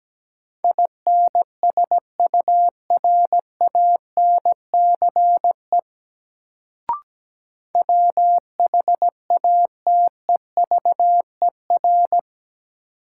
Morse code, heard as a keyed tone, about 17 wpm, 700 Hz. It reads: INSURANCE WHATEVER
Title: Sentences from Top 1100 Words - New Word Drill - 17wpm